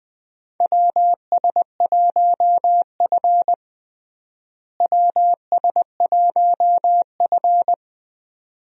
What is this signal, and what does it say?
Morse code, heard as a keyed tone, about 21 wpm, 700 Hz. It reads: WS1F WS1F
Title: USA Call Signs - 4 Characters - Repeated 2x - 20wpm